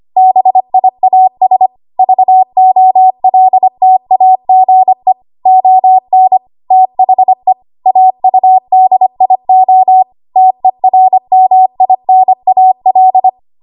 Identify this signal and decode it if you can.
Morse code, heard as a keyed tone, about 25 wpm, 750 Hz. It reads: BIAS VOLTAGE ON THE AUDIO TERMINAL
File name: Archive / 25 WPM / 25WPM